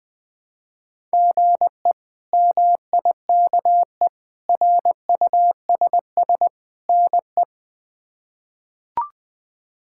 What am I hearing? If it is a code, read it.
Morse code, heard as a keyed tone, about 21 wpm, 700 Hz. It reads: GE MIKE RUSS NE
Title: K1USN SST QSO Practice - 20wpm